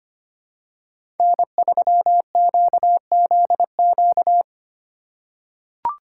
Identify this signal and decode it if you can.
Morse code, heard as a keyed tone, about 25 wpm, 700 Hz. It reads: N3QZQ
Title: USA Call Signs - 5 Characters - 4x Character Spacing - 25wpm